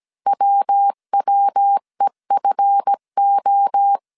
Morse code, heard as a keyed tone, about 17 wpm, 800 Hz. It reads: WWEFO
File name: CW / DET / 1 9 1500C S A 16wpm 2828